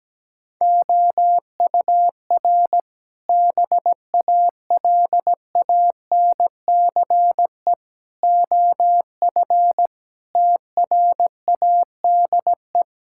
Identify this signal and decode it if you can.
Morse code, heard as a keyed tone, about 17 wpm, 700 Hz. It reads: OUR BALANCE OF TRADE
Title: Sentences from Top 1000 Words - Review Part 6 - 17wpm